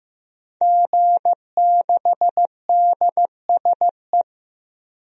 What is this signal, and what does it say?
Morse code, heard as a keyed tone, about 15 wpm, 700 Hz. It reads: G6DSE
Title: DXCC-100 Call Signs - 5 Characters - 15wpm